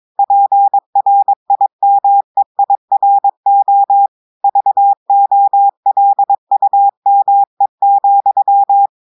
Morse code, heard as a keyed tone, about 22 wpm, 800 Hz. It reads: PRIMEIRO VOLUME,
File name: audios / portugais / portugais texte en clair 1 vitesse 22 mots minutes